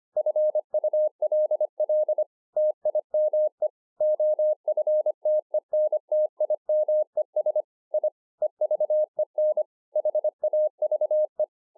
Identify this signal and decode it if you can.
Morse code, heard as a keyed tone, about 25 wpm, 610 Hz. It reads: FULL TIME OFTENTIMES I EVEN HAVE
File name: 2011-04-mp3 / THIS IS NOT MY SHACK 2 25WPM